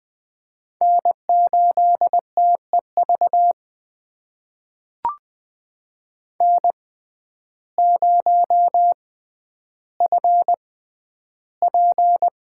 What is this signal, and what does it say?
Morse code, heard as a keyed tone, about 20 wpm, spaced as for 11 wpm, 700 Hz. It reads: N8TEV N0FP
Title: USA Call Signs - 6x Character Spacing - 20wpm